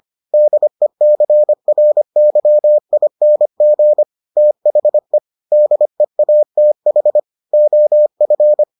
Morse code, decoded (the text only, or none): DECRYING THE DEATH OF